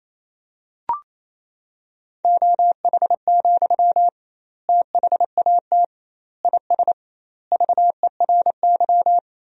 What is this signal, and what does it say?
Morse code, heard as a keyed tone, about 28 wpm, 700 Hz. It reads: OH, THAT IS VERY
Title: 2x Word-Spacing - Sentences from Top 500 Words - 28wpm